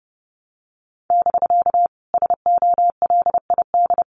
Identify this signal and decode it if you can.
Morse code, heard as a keyed tone, about 31 wpm, 700 Hz. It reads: <BK> SOLID
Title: Common QSO Phrases Encore 30wpm